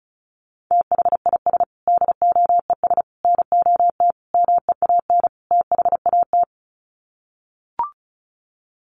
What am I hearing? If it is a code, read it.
Morse code, heard as a keyed tone, about 34 wpm, 700 Hz. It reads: THIS DOES NOT MEAN THAT
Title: Sets of 5 Words Encore2 35wpm